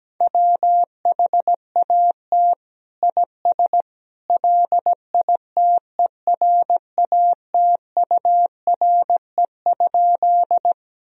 Morse code, heard as a keyed tone, about 17 wpm, 700 Hz. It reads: WHAT IS LITERATURE?
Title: Sentences from Top 1000 Words - Review Part 3 - 17wpm